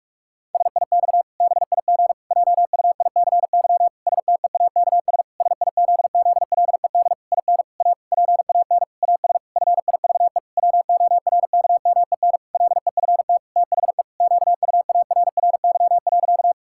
Morse code, heard as a keyed tone, about 46 wpm, 700 Hz. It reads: SIX BIG JUICY STEAKS SIZZLED IN A PAN AS FIVE WORKMEN LEFT THE QUARRY.
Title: Pangram Challenge v2 - 45wpm